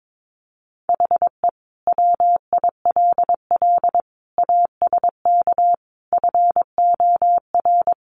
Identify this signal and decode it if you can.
Morse code, heard as a keyed tone, about 22 wpm, 700 Hz. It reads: HE WILL ASK FOR
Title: Sentences from Top 200 Words 22wpm